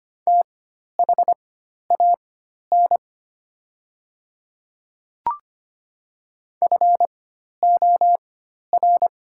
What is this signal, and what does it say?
Morse code, heard as a keyed tone, about 25 wpm, spaced as for 12 wpm, 700 Hz. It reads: THAN FOR
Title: 4x Character Spacing - Top 100 Words - 25wpm